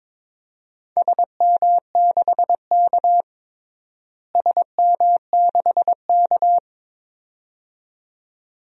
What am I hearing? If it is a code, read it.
Morse code, heard as a keyed tone, about 22 wpm, 700 Hz. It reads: SM6K SM6K
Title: DXCC-100 Call Signs - 4 Characters - Repeated 2x - 22wpm